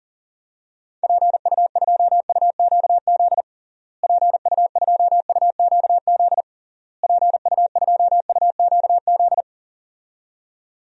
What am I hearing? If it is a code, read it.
Morse code, heard as a keyed tone, about 40 wpm, 700 Hz. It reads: PU2UQZ PU2UQZ PU2UQZ
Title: DXCC-100 Call Signs - 6 Characters - Repeated 3x - 40wpm